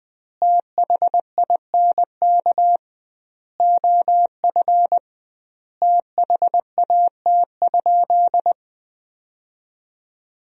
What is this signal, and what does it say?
Morse code, heard as a keyed tone, about 20 wpm, 700 Hz. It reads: THINK OF THAT?
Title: 2x Word-Spacing - Sentences from Top 200 Words - 20wpm